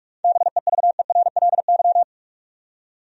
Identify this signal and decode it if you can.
Morse code, heard as a keyed tone, about 46 wpm, 700 Hz. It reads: BEVERLY
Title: USA Common Names - 45wpm